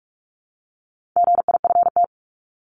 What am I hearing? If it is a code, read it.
Morse code, heard as a keyed tone, about 44 wpm, 700 Hz. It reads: GIFT
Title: Top 751-1000 Words 45wpm